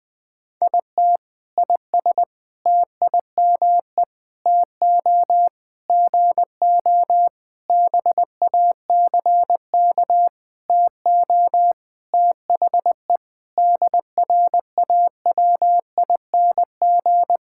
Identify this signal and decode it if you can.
Morse code, heard as a keyed tone, about 20 wpm, 700 Hz. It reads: IT IS TIME TO GO BACK TO THE DRAWING
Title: Sentences with 7-Word Phrases - 20wpm